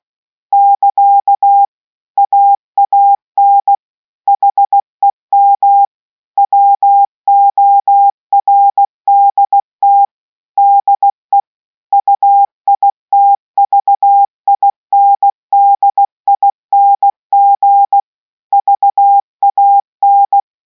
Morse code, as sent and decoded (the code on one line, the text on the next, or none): -.-.- / .- .- -. / .... -- / .-- --- .-. -.. - / -.. / ..- .. - ...- .. -. -.. .. -. --. / ...- .- -.
<KA> AAN HEM WORDT DE UITVINDING VAN